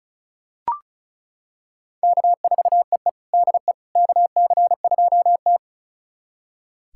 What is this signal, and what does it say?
Morse code, heard as a keyed tone, about 34 wpm, 700 Hz. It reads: K4EE DE KC2T
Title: USA Callsign DE Callsign - 4 Characters - 35wpm